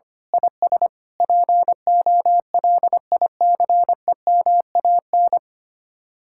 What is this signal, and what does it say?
Morse code, heard as a keyed tone, about 24 wpm, 700 Hz. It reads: IS POLICEMAN
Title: Speed-Racing Common QSO Phrases - 25wpm